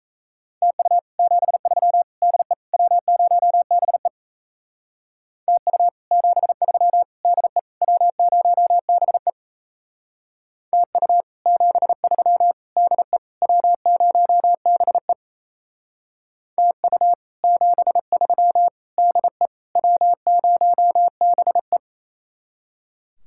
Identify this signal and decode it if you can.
Morse code, heard as a keyed tone, about 35 wpm, 700 Hz. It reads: TU 73 DE W0BE TU 73 DE W0BE TU 73 DE W0BE TU 73 DE W0BE